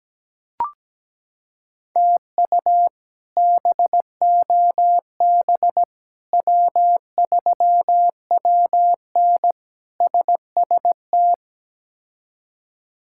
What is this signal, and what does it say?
Morse code, heard as a keyed tone, about 17 wpm, 700 Hz. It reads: TU BOB W3WN SST
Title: K1USN SST QSO Practice - 17wpm